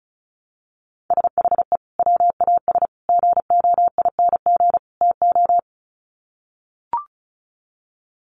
Morse code, heard as a keyed tone, about 34 wpm, 700 Hz. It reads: SHE WAS GOING TO
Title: Sets of 4 Words 35wpm